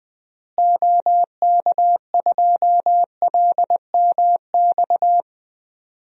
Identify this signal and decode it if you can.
Morse code, heard as a keyed tone, about 20 wpm, 700 Hz. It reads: OK2LMX
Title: DXCC-100 Call Signs - 6 Characters - Repeated 3x - 20wpm